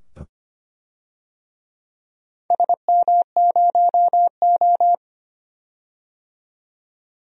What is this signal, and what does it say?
Morse code, heard as a keyed tone, about 25 wpm, 700 Hz. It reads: SM0O
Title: DXCC-100 Call Signs - 4 Characters - 25wpm